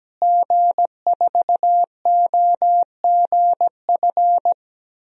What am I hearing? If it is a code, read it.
Morse code, heard as a keyed tone, about 17 wpm, 700 Hz. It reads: G4OGF
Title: DXCC-100 Call Signs - 5 Characters - Repeated 2x - 17wpm